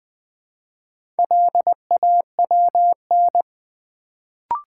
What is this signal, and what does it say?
Morse code, heard as a keyed tone, about 20 wpm, 700 Hz. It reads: LAWN
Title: Build A Word - 1 Letter At A Time - 4-Letter Words - 20wpm